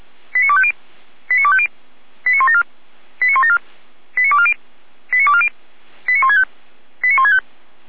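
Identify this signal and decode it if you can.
Five-tone selective call (ZVEI), heard as quick runs of five tones, tones of about 70 ms. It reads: 89250 89250 89174 89174 89240 89240 89165 89165
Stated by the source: ZVEI five-tone call received by SDR, from feuerwehr-orlamuende.org fire brigade archive.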